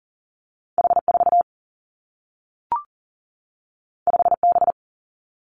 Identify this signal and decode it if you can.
Morse code, heard as a keyed tone, about 39 wpm, 700 Hz. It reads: H4 5B
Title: Two Challenging Letter-Number Combinations 40wpm